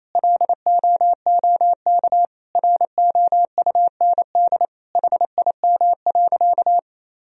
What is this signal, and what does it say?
Morse code, heard as a keyed tone, about 28 wpm, 700 Hz. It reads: LOOK ROUND HIM.
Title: Sentences from Top 1000 Words - Review Part 1 - 28wpm